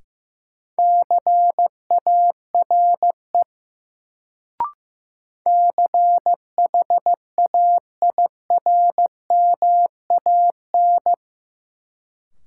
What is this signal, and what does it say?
Morse code, heard as a keyed tone, about 15 wpm, 700 Hz. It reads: CARE CHAIRMAN
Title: Sets of 3 Words - New Word Drill - 15wpm